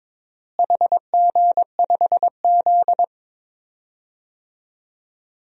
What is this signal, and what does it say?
Morse code, heard as a keyed tone, about 22 wpm, 700 Hz. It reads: HG5Z
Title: DXCC-100 Call Signs - 4 Characters - Repeated 2x - 22wpm